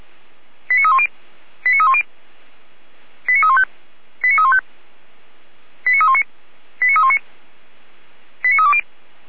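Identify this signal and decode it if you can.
Five-tone selective call (ZVEI), heard as quick runs of five tones, tones of about 70 ms. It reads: 89310 89310 89315 89315 89319 89319 89320
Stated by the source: ZVEI five-tone call 89304 received by SDR.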